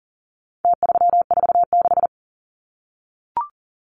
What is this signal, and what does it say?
Morse code, heard as a keyed tone, about 40 wpm, 700 Hz. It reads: T346
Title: Numbers - Contest Style - 4-Digit 40wpm